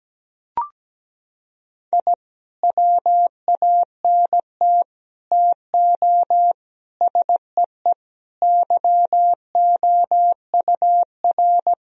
Morse code, heard as a keyed tone, about 17 wpm, 700 Hz. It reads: I WANT TO SEE YOUR
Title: Sentences from Top 1000 Words - Review Part 1 - 17wpm